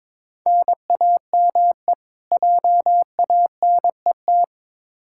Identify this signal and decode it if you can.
Morse code, heard as a keyed tone, about 21 wpm, 700 Hz. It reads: NAME JANET
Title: K1USN SST QSO Practice 22wpm